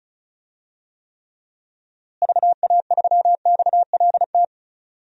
Morse code, heard as a keyed tone, about 34 wpm, 700 Hz. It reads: VA3XLT
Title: DXCC-100 Call Signs - 6 Characters - Repeated 3x - 35wpm